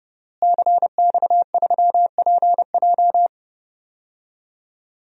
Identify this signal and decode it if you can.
Morse code, heard as a keyed tone, about 31 wpm, 700 Hz. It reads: CX3PJ
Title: DXCC-100 Call Signs - 5 Characters - Repeated 3x - 30wpm